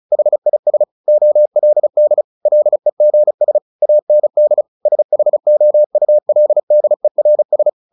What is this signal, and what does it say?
Morse code, heard as a keyed tone, about 34 wpm, 600 Hz. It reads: HIS OLD LEGS AND SHOULDERS
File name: the old man and the sea 2 35wpm